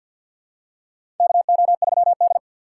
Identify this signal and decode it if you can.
Morse code, heard as a keyed tone, about 52 wpm, 700 Hz. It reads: KO3D
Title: USA Call Signs - 4 Characters - 6x Character Spacing - 50wpm